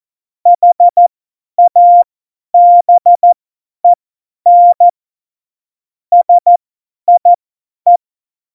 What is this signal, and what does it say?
Morse code, heard as a keyed tone, about 14 wpm, spaced as for 10 wpm, 700 Hz. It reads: HABEN SIE
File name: Morsekurs / Witze / Witz 06. November 2018, 50 F70